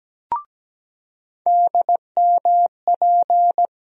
Morse code, heard as a keyed tone, about 17 wpm, 700 Hz. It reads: DMP
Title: USA Call Sign Postfixes - 3 Letters 17wpm